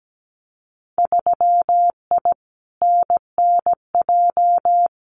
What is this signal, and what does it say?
Morse code, heard as a keyed tone, about 17 wpm, 700 Hz. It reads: 3I NNJ
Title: Winter Field Day Exchanges 17wpm